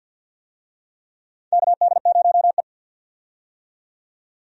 Morse code, heard as a keyed tone, about 52 wpm, 700 Hz. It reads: KD0E
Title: USA Call Signs - 4 Characters - 50wpm